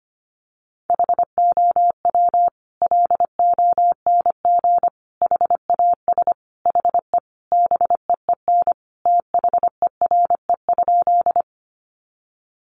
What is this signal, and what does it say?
Morse code, heard as a keyed tone, about 25 wpm, 700 Hz. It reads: HOW LONG HAS HE BEEN THERE?
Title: Sentences from Top 100 Words 25wpm